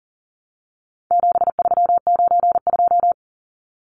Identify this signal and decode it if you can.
Morse code, heard as a keyed tone, about 40 wpm, 700 Hz. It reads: 7392
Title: Numbers - 4-Digit 40wpm